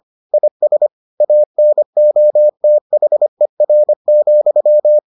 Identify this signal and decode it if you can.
Morse code, heard as a keyed tone, about 25 wpm, 600 Hz. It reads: IS ANOTHER,